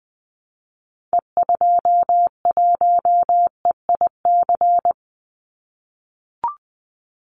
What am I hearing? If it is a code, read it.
Morse code, heard as a keyed tone, about 20 wpm, 700 Hz. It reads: E21EIC
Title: Call Signs - 20wpm